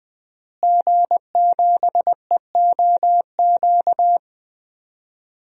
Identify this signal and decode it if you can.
Morse code, heard as a keyed tone, about 21 wpm, 700 Hz. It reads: G7EOQ
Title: DXCC-100 Call Signs - 5 Characters - Repeated 2x - 20wpm